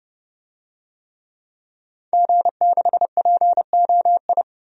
Morse code, heard as a keyed tone, about 30 wpm, 700 Hz. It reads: G6POI
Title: DXCC-100 Call Signs - 5 Characters - 30wpm